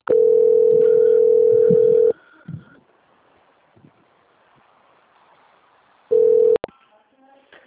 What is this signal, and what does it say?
Ringback tone (North America), 440 + 480 Hz.
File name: out- - - - - .49256